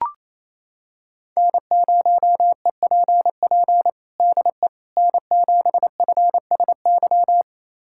Morse code, heard as a keyed tone, about 28 wpm, 700 Hz. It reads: N0EPP DE N7FSY
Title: USA Callsign DE Callsign - 5 Characters - 28wpm